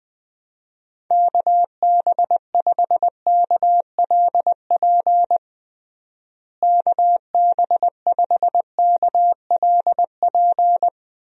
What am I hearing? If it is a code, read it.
Morse code, heard as a keyed tone, about 20 wpm, 700 Hz. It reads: KB5KLP KB5KLP
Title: USA Call Signs - 6 Characters - Repeated 2x - 20wpm